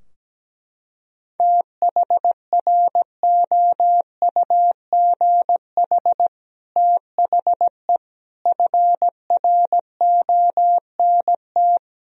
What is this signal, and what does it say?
Morse code, heard as a keyed tone, about 17 wpm, 700 Hz. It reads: THROUGH THE FRONT
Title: Sentences from Top 1000 Words - Review Part 2 - 17wpm